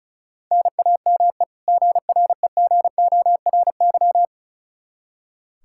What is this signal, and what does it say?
Morse code, heard as a keyed tone, about 36 wpm, 700 Hz. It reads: NAME GREGORY
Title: Practice QSO - Exchange 1 - Beginning - 35wpm